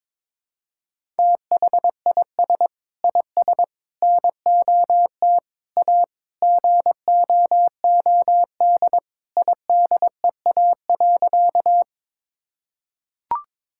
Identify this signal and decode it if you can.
Morse code, heard as a keyed tone, about 21 wpm, 700 Hz. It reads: THIS IS NOT A GOOD IDEA.
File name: practice / 22wpm / Sentences with 4-Word Phrases - 22wpm